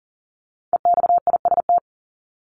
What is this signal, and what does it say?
Morse code, heard as a keyed tone, about 39 wpm, 700 Hz. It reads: EXIST
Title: Top 751-1000 Words 40wpm